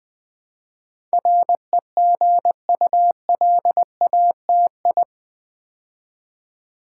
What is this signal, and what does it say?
Morse code, heard as a keyed tone, about 20 wpm, 700 Hz. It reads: REGULATI
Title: Build A Word - 4 Letters At A Time - 10-Letter Words - 20wpm